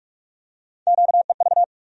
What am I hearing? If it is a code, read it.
Morse code, heard as a keyed tone, about 48 wpm, 700 Hz. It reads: QEV
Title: USA Call Sign Postfixes - 3 Letters 45wpm